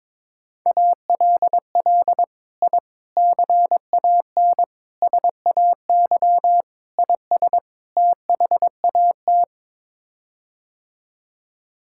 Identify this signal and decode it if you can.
Morse code, heard as a keyed tone, about 21 wpm, 700 Hz. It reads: ALL I CAN SAY IS THAT
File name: practice / 22wpm / Sets of 6 Words - 22wpm